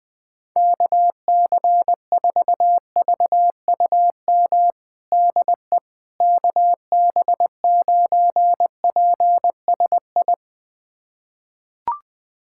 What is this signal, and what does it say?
Morse code, heard as a keyed tone, about 20 wpm, 700 Hz. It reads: KC4VUM DE KB9PSI
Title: USA Callsign DE Callsign - 6 Characters - 20wpm